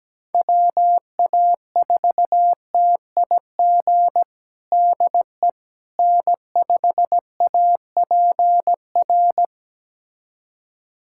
Morse code, heard as a keyed tone, about 17 wpm, 700 Hz. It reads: WA4TIG DE N5APR
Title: USA Callsign DE Callsign 17wpm